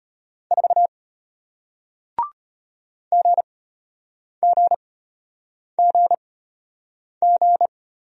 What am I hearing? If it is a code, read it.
Morse code, heard as a keyed tone, about 34 wpm, spaced as for 8 wpm, 700 Hz. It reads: 4 GGGG